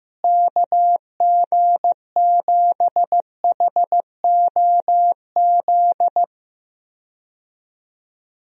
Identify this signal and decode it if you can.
Morse code, heard as a keyed tone, about 16 wpm, 700 Hz. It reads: KG7HOZ
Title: USA Call Signs - 6 Characters - Repeated 3x - 15wpm